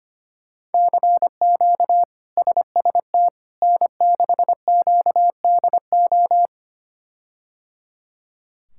Morse code, heard as a keyed tone, about 25 wpm, 700 Hz. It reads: CQ SST N6QDO
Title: K1USN SST QSO Practice - 25wpm